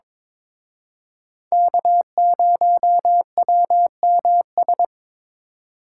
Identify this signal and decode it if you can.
Morse code, heard as a keyed tone, about 26 wpm, 700 Hz. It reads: K0WMS
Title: Speed Racing - USA 1x3 Call Signs 22wpm